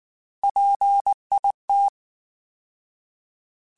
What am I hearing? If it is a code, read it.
Morse code, heard as a keyed tone, about 19 wpm, 800 Hz. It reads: PIT
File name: Alphabetical / Alphabetical mp3s / Ps @18